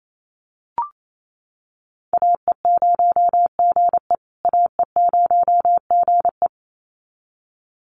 Morse code, heard as a keyed tone, about 28 wpm, 700 Hz. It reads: AE0GE AE0GE
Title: USA Call Signs Repeated 2x 28wpm